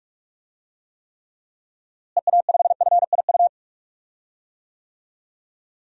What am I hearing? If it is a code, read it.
Morse code, heard as a keyed tone, about 46 wpm, 700 Hz. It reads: EA5FIU
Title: DXCC-100 Call Signs - 6 Characters - Repeated 2x - 45wpm